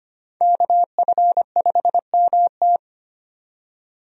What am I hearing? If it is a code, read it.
Morse code, heard as a keyed tone, about 25 wpm, 700 Hz. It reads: KF5MT
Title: USA Call Signs - 5 Characters - 25wpm